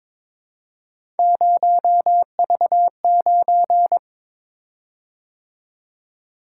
Morse code, heard as a keyed tone, about 22 wpm, 700 Hz. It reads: 0V9
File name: practice / 22wpm / Three Letter-Number Combinations - 22wpm